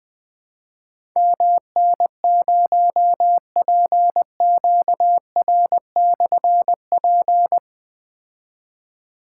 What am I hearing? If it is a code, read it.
Morse code, heard as a keyed tone, about 20 wpm, 700 Hz. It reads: MN0PQR/P
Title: Difficult Call Signs 20wpm